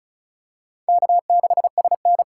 Morse code, heard as a keyed tone, about 36 wpm, 700 Hz. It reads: K6SN
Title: USA Call Signs - 4 Characters - 4x Character Spacing - 35wpm